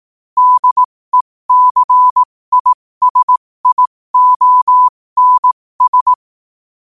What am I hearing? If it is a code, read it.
Morse code, heard as a keyed tone, about 18 wpm, spaced as for 15 wpm, 1000 Hz. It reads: DECISIONS
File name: qotdfiles / 15wpm